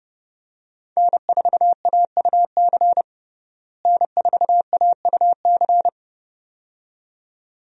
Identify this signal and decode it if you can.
Morse code, heard as a keyed tone, about 30 wpm, 700 Hz. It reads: N4AUC N4AUC
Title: USA Call Signs - 5 Characters - Repeated 2x - 30wpm